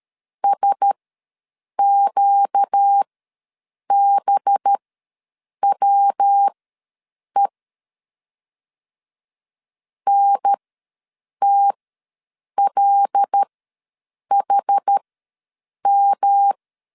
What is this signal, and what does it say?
Morse code, heard as a keyed tone, about 13 wpm, spaced as for 7 wpm, 800 Hz. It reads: SQBWE NTLHM
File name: LEZIONI / 5 LBVC / 5 LB ws 12wpm 3034